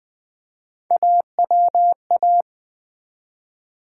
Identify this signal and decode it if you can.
Morse code, heard as a keyed tone, about 20 wpm, 700 Hz. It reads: AWA
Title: Build A Word - 1 Letter At A Time - 5-Letter Words - 20wpm